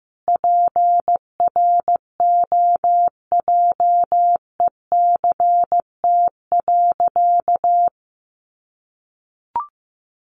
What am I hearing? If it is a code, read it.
Morse code, heard as a keyed tone, about 15 wpm, 700 Hz. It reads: PROJECT.
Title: Sentences from Top 1400 Words 15wpm